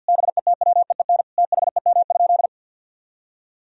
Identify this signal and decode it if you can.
Morse code, heard as a keyed tone, about 48 wpm, 700 Hz. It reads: BETWEEN THEM?
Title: Sentences from Top 1000 Words - Review Part 4 - 50wpm